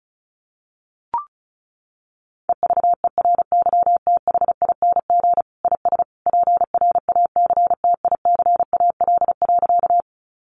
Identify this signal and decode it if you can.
Morse code, heard as a keyed tone, about 34 wpm, 700 Hz. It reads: EVERYTHING IS PRACTICAL.
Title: Sentences from Top 1400 Words 35wpm